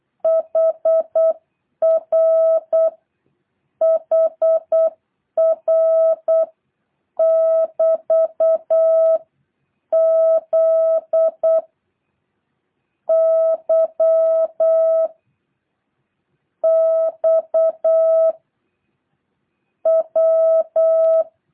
Morse code, heard as a keyed tone, about 8 wpm, 650 Hz. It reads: HR HR <BT> Z Y X W
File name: jh8cbh